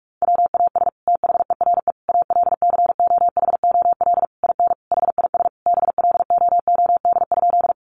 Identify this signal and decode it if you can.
Morse code, heard as a keyed tone, about 44 wpm, 700 Hz. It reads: WAS THERE ALCOHOL IN HIS BLOOD?
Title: Sentences from Top 1400 Words 45wpm